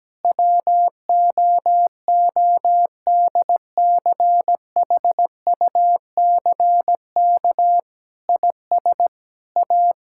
Morse code, heard as a keyed tone, about 17 wpm, 700 Hz. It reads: WOODCHUCK IS A
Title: Pangram Challenge v2 - 17wpm